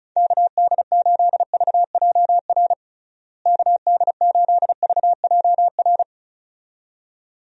Morse code, heard as a keyed tone, about 36 wpm, 700 Hz. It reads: KD8VJR KD8VJR
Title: USA Call Signs - 6 Characters - Repeated 2x - 35wpm